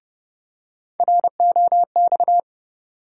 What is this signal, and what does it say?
Morse code, heard as a keyed tone, about 30 wpm, 700 Hz. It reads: ROX
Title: USA Call Sign Postfixes - 3 Letters 30wpm